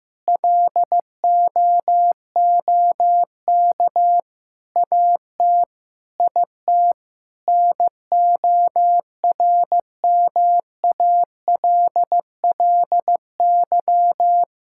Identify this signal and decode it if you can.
Morse code, heard as a keyed tone, about 15 wpm, 700 Hz. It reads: LOOK AT IT NORMALLY
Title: Sentences from Top 1000 Words - Review Part 6 - 15wpm